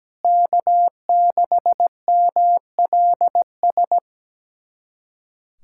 Morse code, heard as a keyed tone, about 17 wpm, 700 Hz. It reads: K6MLS
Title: USA Callsign DE Callsign 17wpm